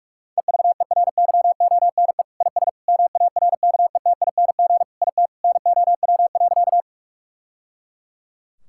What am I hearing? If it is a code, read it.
Morse code, heard as a keyed tone, about 43 wpm, 700 Hz. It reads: EVERYONE IS MARKETING IT NOW.
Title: Sentences from Top 1000 Words - Review Part 5 - 45wpm